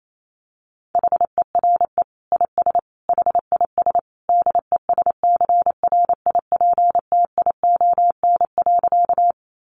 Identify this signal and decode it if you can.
Morse code, heard as a keyed tone, about 28 wpm, 700 Hz. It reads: HERE IS HIS DESCRIPTION.
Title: Sentences from Top 1300 Words 28wpm